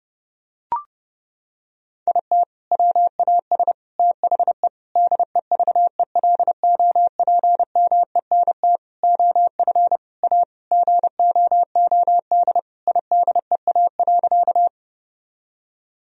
Morse code, heard as a keyed tone, about 30 wpm, 700 Hz. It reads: IT WAS THE DEVELOPMENT OF A GOOD IDEA.
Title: Sentences from Top 1000 Words - Review Part 4 - 30wpm